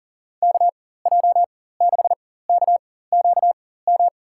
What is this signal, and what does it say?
Morse code, heard as a keyed tone, about 40 wpm, spaced as for 19 wpm, 700 Hz. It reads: KJ6KQM
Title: USA Call Signs - 4x Character Spacing - 40wpm